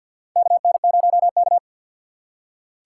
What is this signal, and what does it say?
Morse code, heard as a keyed tone, about 51 wpm, 700 Hz. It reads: KN0K
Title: USA Call Signs - 4 Characters - 3x Character Spacing - 50wpm